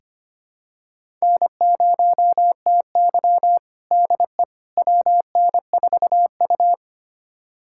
Morse code, heard as a keyed tone, about 25 wpm, 700 Hz. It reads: N0TY DE WN4U
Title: USA Callsign DE Callsign - 4 Characters - 25wpm